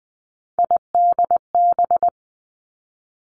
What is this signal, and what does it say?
Morse code, heard as a keyed tone, about 20 wpm, 700 Hz. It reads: IDB
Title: Three Challenging Letter-Number Combinations 20wpm